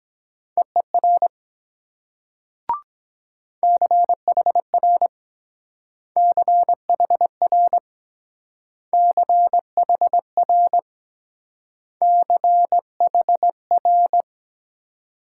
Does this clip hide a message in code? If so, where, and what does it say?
Yes: Morse code from 0.6 to 14.2 s, heard as a keyed tone, about 23 wpm, 700 Hz. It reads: EER CHR CHR CHR CHR